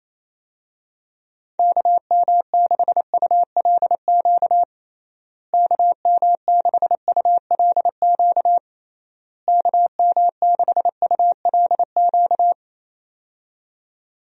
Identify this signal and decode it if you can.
Morse code, heard as a keyed tone, about 27 wpm, 700 Hz. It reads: KM6ULQ KM6ULQ KM6ULQ
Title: USA Call Signs - 6 Characters - Repeated 3x - 28wpm